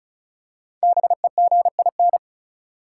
Morse code, heard as a keyed tone, about 36 wpm, 700 Hz. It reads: BEGIN
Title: Five Letter Words - 35wpm